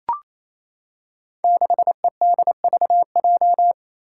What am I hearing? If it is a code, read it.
Morse code, heard as a keyed tone, about 28 wpm, 700 Hz. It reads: 6EDVJ
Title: Code Groups - 5 Letter:Numbers - 28wpm